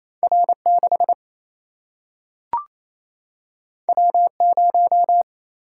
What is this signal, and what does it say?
Morse code, heard as a keyed tone, about 28 wpm, 700 Hz. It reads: R 6 W 0
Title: Single-Letter Single-Number 28wpm